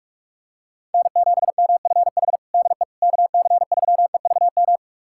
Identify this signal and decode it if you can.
Morse code, heard as a keyed tone, about 43 wpm, 700 Hz. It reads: N7MFH DE KC3EVM
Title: USA Callsign DE Callsign 45wpm